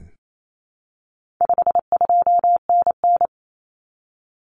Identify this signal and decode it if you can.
Morse code, heard as a keyed tone, about 28 wpm, 700 Hz. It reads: 52NN
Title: Common Cut Numbers - 4-Digit 28wpm